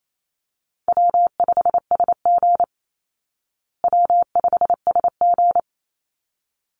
Morse code, heard as a keyed tone, about 28 wpm, 700 Hz. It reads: W5SG W5SG
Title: Call Signs Repeated 3x - 28wpm